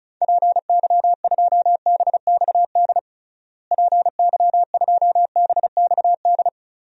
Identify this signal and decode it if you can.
Morse code, heard as a keyed tone, about 36 wpm, 700 Hz. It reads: PY2BXD PY2BXD
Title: DXCC-100 Call Signs - 6 Characters - Repeated 2x - 35wpm